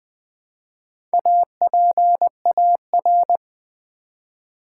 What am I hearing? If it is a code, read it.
Morse code, heard as a keyed tone, about 20 wpm, 700 Hz. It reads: APAR
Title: Build A Word - 1 Letter At A Time - 5-Letter Words - 20wpm